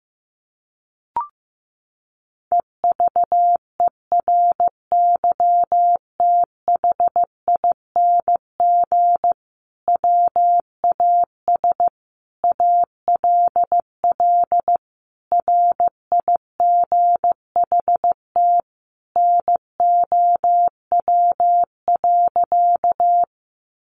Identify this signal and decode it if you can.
Morse code, heard as a keyed tone, about 15 wpm, 700 Hz. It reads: EVERYTHING WAS ALL RIGHT NOW.
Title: Sentences from Top 700 Words 15wpm